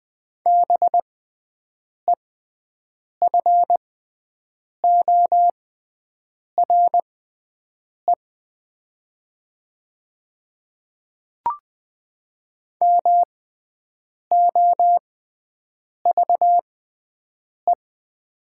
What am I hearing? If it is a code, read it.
Morse code, heard as a keyed tone, about 20 wpm, spaced as for 7 wpm, 700 Hz. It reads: BEFORE MOVE
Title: 6x Character Spacing - Top 200 Words - 20wpm